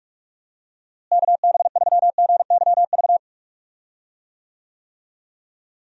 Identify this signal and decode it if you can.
Morse code, heard as a keyed tone, about 43 wpm, 700 Hz. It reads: KB3GYV
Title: USA Call Signs - 6 Characters - Repeated 2x - 45wpm